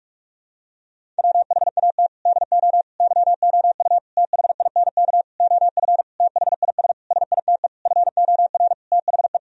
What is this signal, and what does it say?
Morse code, heard as a keyed tone, about 48 wpm, 700 Hz. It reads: WHAT DO YOU THINK OF THIS SITE FOR THE
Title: Sentences from Top 1000 Words - Review Part 1 - 45wpm